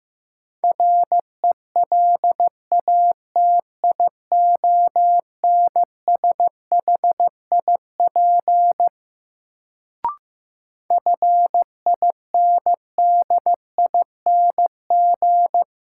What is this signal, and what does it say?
Morse code, heard as a keyed tone, about 15 wpm, 700 Hz. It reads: RELATIONSHIP FINDING
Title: Sentences from Top 1100 Words - New Word Drill - 15wpm